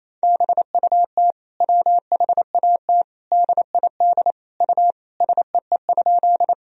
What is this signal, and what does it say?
Morse code, heard as a keyed tone, about 28 wpm, 700 Hz. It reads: BUT WHAT DID U SEE?
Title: Sentences from Top 100 Words and Abbreviations 28wpm